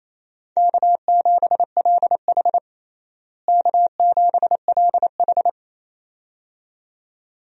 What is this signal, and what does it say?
Morse code, heard as a keyed tone, about 27 wpm, 700 Hz. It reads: K7LH K7LH
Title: USA Call Signs - 4 Characters - Repeated 2x - 28wpm